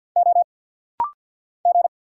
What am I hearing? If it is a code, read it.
Morse code, heard as a keyed tone, about 48 wpm, 700 Hz. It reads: O G
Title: Single Letters - Rapid-Fire 50wpm